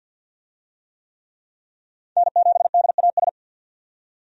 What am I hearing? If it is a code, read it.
Morse code, heard as a keyed tone, about 48 wpm, 700 Hz. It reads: N7DAS
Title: USA Call Signs - 5 Characters - 50wpm